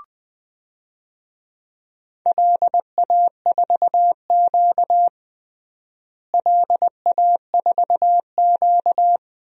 Morse code, heard as a keyed tone, about 21 wpm, 700 Hz. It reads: LA4Q LA4Q
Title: DXCC-100 Call Signs - 4 Characters - Repeated 2x - 20wpm